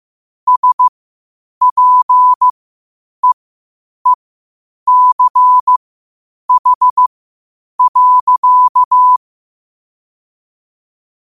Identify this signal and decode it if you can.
Morse code, heard as a keyed tone, about 15 wpm, spaced as for 8 wpm, 1000 Hz. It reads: SPEECH.
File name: qotdfiles / 8wpm